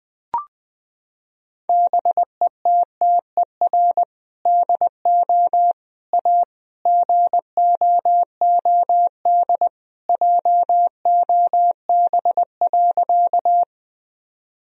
Morse code, heard as a keyed tone, about 21 wpm, 700 Hz. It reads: BETTER DO A GOOD JOB.
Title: Sentences from Top 1000 Words - Review Part 3 - 20wpm